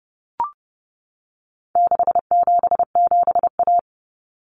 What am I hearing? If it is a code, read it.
Morse code, heard as a keyed tone, about 30 wpm, 700 Hz. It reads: 677A
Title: Common Cut Numbers - 4-Digit 30wpm